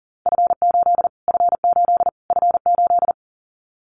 Morse code, heard as a keyed tone, about 37 wpm, 700 Hz. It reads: F8 F8 F8
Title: Callsign Prefixes for Top 25 DXCC - Repeated 3x - 40wpm